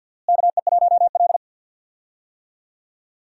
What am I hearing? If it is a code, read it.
Morse code, heard as a keyed tone, about 51 wpm, 690 Hz. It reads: KE1L
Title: USA Call Signs - 4 Characters - Repeated 2x - 50wpm